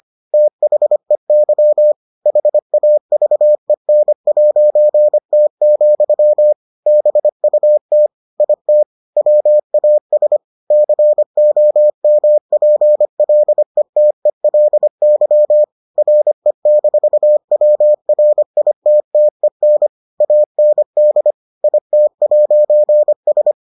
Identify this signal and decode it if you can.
Morse code, heard as a keyed tone, about 25 wpm, 600 Hz. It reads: THEY HAVEN'T, BUT IT WAS COMPLETELY RE-WRITTEN AND IT'S